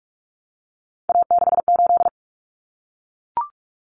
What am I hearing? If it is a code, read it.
Morse code, heard as a keyed tone, about 44 wpm, 700 Hz. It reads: A68
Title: Common Cut Numbers - 3-Digit 45wpm